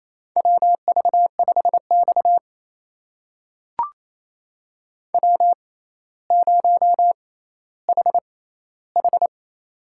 Morse code, heard as a keyed tone, about 28 wpm, spaced as for 14 wpm, 700 Hz. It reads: WV5X W0HH
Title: USA Call Signs - 6x Character Spacing - 28wpm